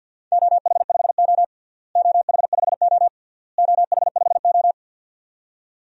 Morse code, heard as a keyed tone, about 52 wpm, 700 Hz. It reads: OH5O OH5O OH5O
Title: DXCC-100 Call Signs - 4 Characters - Repeated 3x - 50wpm